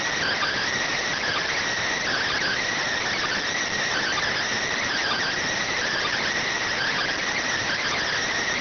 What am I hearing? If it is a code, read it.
Emergency Alert System (SAME) tones (header bursts), heard as harsh buzzing bursts. It is heard from 0.8 to 8.1 s.